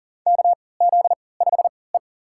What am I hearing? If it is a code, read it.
Morse code, heard as a keyed tone, about 38 wpm, spaced as for 22 wpm, 700 Hz. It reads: KZ5E
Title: USA Call Signs - 3x Character Spacing - 40wpm